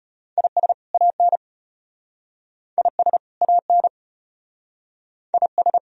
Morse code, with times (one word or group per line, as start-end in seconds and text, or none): 0.4-0.7 IS
0.9-1.4 AN
2.8-3.2 IS
3.4-3.9 AN
5.3-5.8 IS